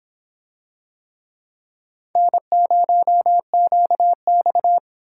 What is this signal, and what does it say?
Morse code, heard as a keyed tone, about 24 wpm, 700 Hz. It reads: N0QX